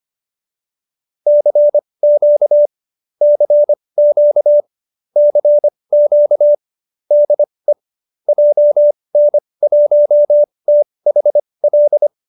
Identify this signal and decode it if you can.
Morse code, heard as a keyed tone, about 25 wpm, spaced as for 20 wpm, 590 Hz. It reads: CQ CQ CQ DE JN1THL
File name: QSO309 20